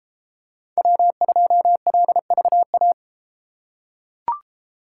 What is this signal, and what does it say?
Morse code, heard as a keyed tone, about 32 wpm, 700 Hz. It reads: W2LVA